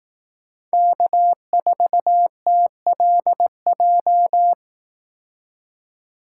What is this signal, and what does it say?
Morse code, heard as a keyed tone, about 18 wpm, 700 Hz. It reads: K4TLJ
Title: Speed Racing - USA Call Signs 15wpm